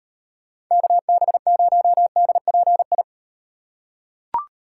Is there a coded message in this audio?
Yes: Morse code from 0.7 to 3.0 s, heard as a keyed tone, about 38 wpm, 700 Hz. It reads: KB0DPI